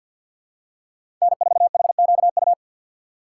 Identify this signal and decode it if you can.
Morse code, heard as a keyed tone, about 52 wpm, 700 Hz. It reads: N4HQU
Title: USA Call Signs - 2x Character Spacing - 50wpm